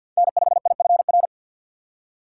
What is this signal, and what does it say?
Morse code, heard as a keyed tone, about 52 wpm, 700 Hz. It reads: N5IFR
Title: USA Call Signs - 5 Characters - 4x Character Spacing - 50wpm